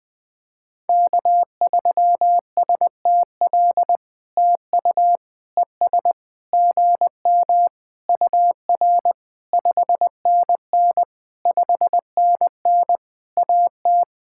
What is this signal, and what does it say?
Morse code, heard as a keyed tone, about 20 wpm, 700 Hz. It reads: K3STL TU ES GM UR 5NN 5NN AT
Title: POTA QSO Practice - 20wpm